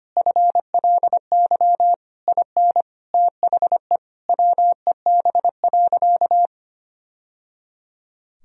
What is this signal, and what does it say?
Morse code, heard as a keyed tone, about 25 wpm, 700 Hz. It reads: FLY IN THE WEB.
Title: Sentences from Top 1000 Words - Review Part 4 - 25wpm